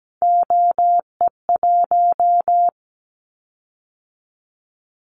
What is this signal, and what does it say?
Morse code, heard as a keyed tone, about 17 wpm, 700 Hz. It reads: OE1
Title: Callsign Prefixes for Top 50 DXCC - 17wpm